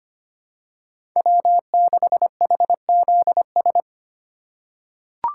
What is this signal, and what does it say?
Morse code, heard as a keyed tone, about 25 wpm, 700 Hz. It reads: W6HZS
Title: USA Call Signs - 8x Character Spacing - 25wpm